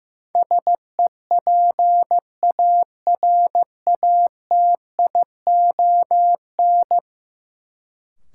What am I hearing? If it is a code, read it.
Morse code, heard as a keyed tone, about 15 wpm, 700 Hz. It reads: SEPARATION
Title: Sets of 5 Words - New Word Drill - 15wpm